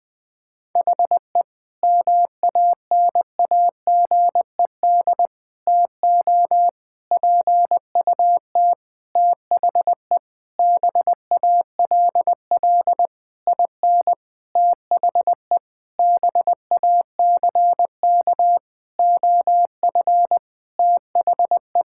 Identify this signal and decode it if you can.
Morse code, heard as a keyed tone, about 19 wpm, 700 Hz. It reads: HE MANAGED TO PUT THE BALL IN THE BACK OF THE
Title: Sentences with 9-Word Phrases - 20wpm